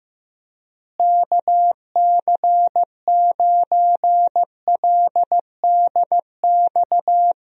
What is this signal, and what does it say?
Morse code, heard as a keyed tone, about 15 wpm, 700 Hz. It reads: KC9LDX
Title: USA Call Signs - 6 Characters - Repeated 3x - 15wpm